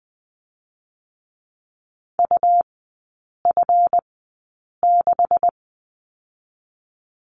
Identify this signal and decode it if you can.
Morse code, heard as a keyed tone, about 20 wpm, spaced as for 8 wpm, 700 Hz. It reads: UF6
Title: 138 Lesson 23e - Letter f - Three Characters 20wpm